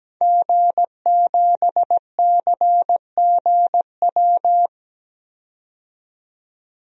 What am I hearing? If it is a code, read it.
Morse code, heard as a keyed tone, about 17 wpm, 700 Hz. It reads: G7CGW
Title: DXCC-100 Call Signs - 5 Characters - 17wpm